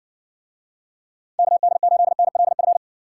Morse code, heard as a keyed tone, about 60 wpm, 700 Hz. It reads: KD8NLF